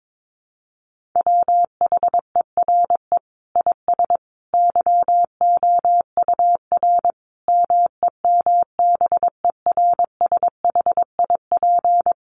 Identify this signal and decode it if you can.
Morse code, heard as a keyed tone, about 22 wpm, 700 Hz. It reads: WHERE IS YOUR MEMBERSHIP
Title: Sentences from Top 1400 Words 22wpm